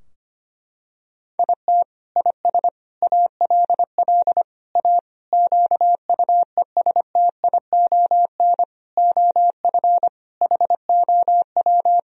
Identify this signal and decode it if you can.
Morse code, heard as a keyed tone, about 25 wpm, 700 Hz. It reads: IT IS ALL A QUESTION OF HOW
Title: Sentences from Top 1000 Words - Review Part 3 - 25wpm